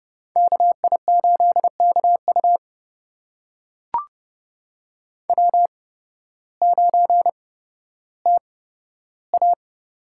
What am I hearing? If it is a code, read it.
Morse code, heard as a keyed tone, about 32 wpm, spaced as for 14 wpm, 700 Hz. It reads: KI8KU W9TA
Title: USA Call Signs - 5 Characters - 8x Character Spacing - 30wpm